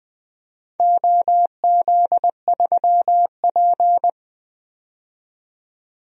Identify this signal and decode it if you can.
Morse code, heard as a keyed tone, about 20 wpm, 700 Hz. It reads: OZ3P
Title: DXCC-100 Call Signs - 4 Characters - 20wpm